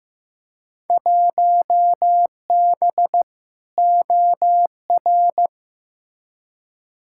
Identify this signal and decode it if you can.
Morse code, heard as a keyed tone, about 15 wpm, 700 Hz. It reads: 1B OR
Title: Field Day Exchanges Encore - 15wpm